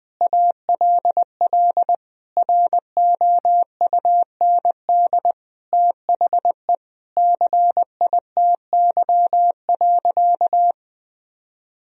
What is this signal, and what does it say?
Morse code, heard as a keyed tone, about 20 wpm, 700 Hz. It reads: ALL ROUND THE CITY.
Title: Sentences from Top 1000 Words - Review Part 1 - 20wpm